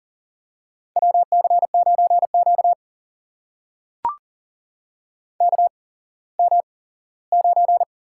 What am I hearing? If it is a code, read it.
Morse code, heard as a keyed tone, about 38 wpm, spaced as for 19 wpm, 700 Hz. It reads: WC9Q KM9
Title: USA Call Signs - 4 Characters - 8x Character Spacing - 40wpm